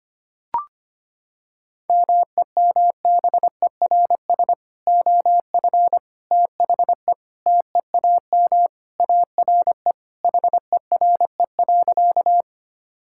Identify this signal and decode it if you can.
Morse code, heard as a keyed tone, about 25 wpm, 700 Hz. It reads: MEMBERS OF THE TEAM ARE HERE.
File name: practice / 25wpm / Sentences with 3-Word Phrases - 25wpm